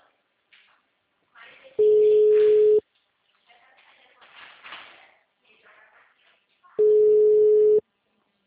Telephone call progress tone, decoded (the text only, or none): ringback tone